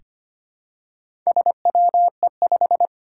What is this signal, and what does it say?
Morse code, heard as a keyed tone, about 25 wpm, 700 Hz. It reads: SWE5
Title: Four Challenging Letter-Number Combinations 25wpm